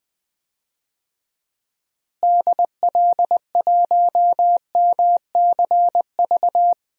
Morse code, heard as a keyed tone, about 20 wpm, 700 Hz. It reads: DL1MCV
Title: DXCC-100 Call Signs - 6 Characters - Repeated 3x - 20wpm